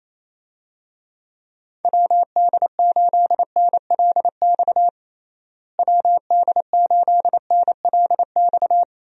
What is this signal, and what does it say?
Morse code, heard as a keyed tone, about 28 wpm, 700 Hz. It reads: WD8NLX WD8NLX
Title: USA Call Signs - 6 Characters - Repeated 2x - 28wpm